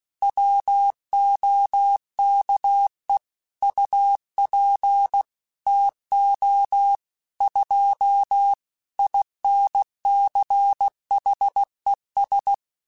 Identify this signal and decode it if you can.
Morse code, heard as a keyed tone, about 16 wpm, 800 Hz. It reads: WOKE UP TO 2 INCHES
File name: Weather Sets / Weather Sets-mp3s / Weather 3 17 @15